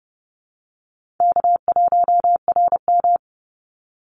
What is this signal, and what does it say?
Morse code, heard as a keyed tone, about 30 wpm, 700 Hz. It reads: K1RM
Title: Easy Call Signs - 30wpm